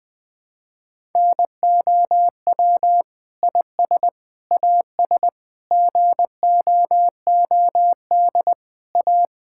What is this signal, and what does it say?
Morse code, heard as a keyed tone, about 21 wpm, 700 Hz. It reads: NOW IS AS GOOD A
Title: Sentences with 7-Word Phrases - Spoken Beginning - 20wpm